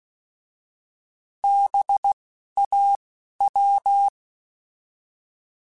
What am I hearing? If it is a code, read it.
Morse code, heard as a keyed tone, about 16 wpm, spaced as for 11 wpm, 800 Hz. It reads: BAW
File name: Alphabetical / Alphabetical mp3s / Bs @10